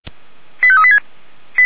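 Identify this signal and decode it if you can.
Five-tone selective call (ZVEI), heard as quick runs of five tones, tones of about 70 ms. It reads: 85386
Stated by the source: ZVEI five-tone call received on SDR.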